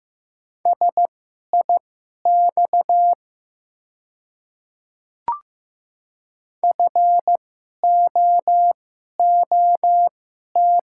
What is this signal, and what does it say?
Morse code, heard as a keyed tone, about 15 wpm, spaced as for 11 wpm, 700 Hz. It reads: SIX FOOT
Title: 2x Character Spacing - Top 500 Words - 15wpm